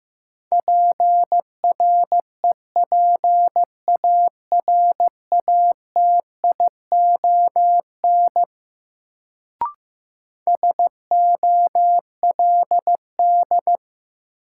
Morse code, heard as a keyed tone, about 15 wpm, 700 Hz. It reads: PREPARATION SOLD
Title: Sentences from Top 1400 Words - New Word Drill Alt - 15wpm